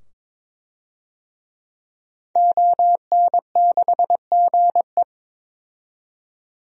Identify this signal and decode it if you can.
Morse code, heard as a keyed tone, about 22 wpm, 700 Hz. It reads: ON6GE
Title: DXCC-100 Call Signs - 5 Characters - 22wpm